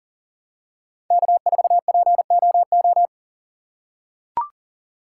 Morse code, heard as a keyed tone, about 40 wpm, 700 Hz. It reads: K4POO
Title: USA Call Signs - 5 Characters - 2x Character Spacing - 40wpm